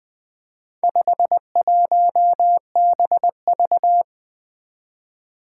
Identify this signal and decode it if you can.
Morse code, heard as a keyed tone, about 20 wpm, 700 Hz. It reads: 51BV
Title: Four Challenging Letter-Number Combinations 20wpm